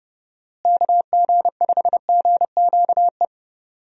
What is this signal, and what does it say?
Morse code, heard as a keyed tone, about 32 wpm, 700 Hz. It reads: KG5GQE
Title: USA Call Signs - 6 Characters - Repeated 3x - 30wpm